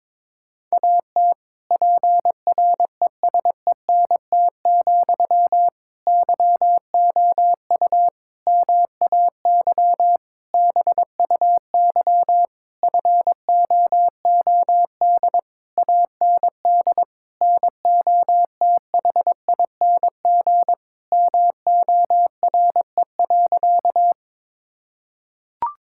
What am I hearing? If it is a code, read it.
Morse code, heard as a keyed tone, about 22 wpm, 700 Hz. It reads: AT PRESENT, YOU MAY BUY FOOD AND NOTHING MORE.
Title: Sentences from Top 1000 Words - Review Part 5 - 22wpm